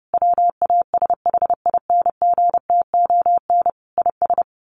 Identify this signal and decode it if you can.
Morse code, heard as a keyed tone, about 30 wpm, 700 Hz. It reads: WASHINGTON IS
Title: Sets of 5 Words Encore2 30wpm